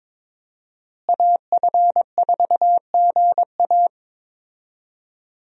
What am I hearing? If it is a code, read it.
Morse code, heard as a keyed tone, about 22 wpm, 700 Hz. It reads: AF4GA
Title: USA Call Signs - 5 Characters - Repeated 2x - 22wpm